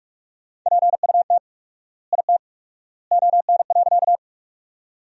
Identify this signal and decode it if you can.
Morse code, heard as a keyed tone, about 46 wpm, 700 Hz. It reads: PUT IT ON.
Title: 4x Word-Spacing - Sentences from Top 200 Words - 45wpm